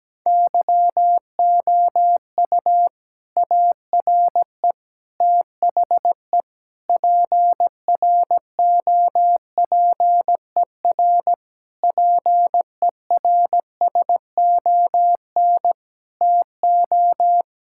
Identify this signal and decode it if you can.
Morse code, heard as a keyed tone, about 17 wpm, 700 Hz. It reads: YOU ARE THE PROPER PERSON TO
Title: Sentences from Top 1000 Words - Review Part 5 - 17wpm